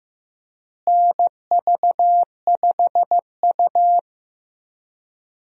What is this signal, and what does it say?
Morse code, heard as a keyed tone, about 16 wpm, 700 Hz. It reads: NV5U
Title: USA Call Signs - 4 Characters - Repeated 3x - 15wpm